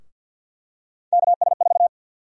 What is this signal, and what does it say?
Morse code, heard as a keyed tone, about 52 wpm, 700 Hz. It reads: KS4
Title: USA Call Sign Prefixes - Repeated 2x - 50wpm